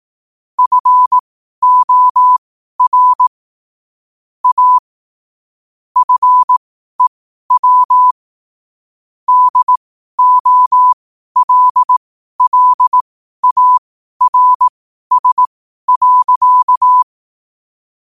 Morse code, heard as a keyed tone, about 18 wpm, spaced as for 12 wpm, 1000 Hz. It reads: FOR A FEW DOLLARS.
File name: qotdfiles / 12wpm